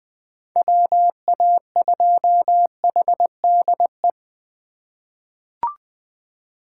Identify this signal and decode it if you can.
Morse code, heard as a keyed tone, about 21 wpm, 700 Hz. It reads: WA2HDE
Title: USA Call Signs - 6 Characters - 2x Character Spacing - 20wpm